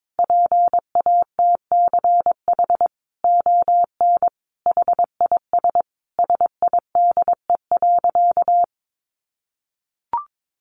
Morse code, heard as a keyed tone, about 22 wpm, 700 Hz. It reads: PATCH ON HIS SIDE.
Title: A Sound Of Thunder 22wpm